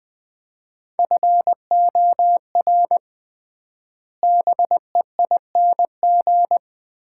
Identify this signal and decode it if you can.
Morse code, heard as a keyed tone, about 20 wpm, 700 Hz. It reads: FOR BEING
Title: 3x Word-Spacing - Sets of 4 Words - 20wpm